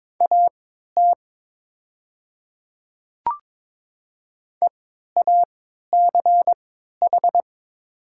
Morse code, heard as a keyed tone, about 22 wpm, spaced as for 12 wpm, 700 Hz. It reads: AT EACH
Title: 3x Character Spacing - Top 100 Words - 22wpm